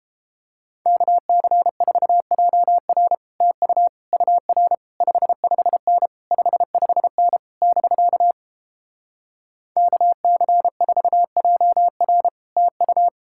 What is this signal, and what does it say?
Morse code, heard as a keyed tone, about 29 wpm, 700 Hz. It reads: KC4JR TU UR 55N 55N <BK> KC4JR TU